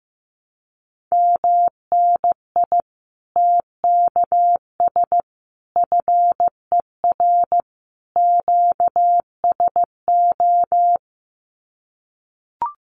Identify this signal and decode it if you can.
Morse code, heard as a keyed tone, about 15 wpm, 700 Hz. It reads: MNI TKS FER QSO
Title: Common QSO Phrases 15wpm